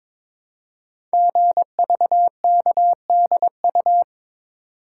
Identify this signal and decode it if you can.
Morse code, heard as a keyed tone, about 22 wpm, 700 Hz. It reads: GVKDU
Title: Code Groups - 5 Letter:Numbers - 22wpm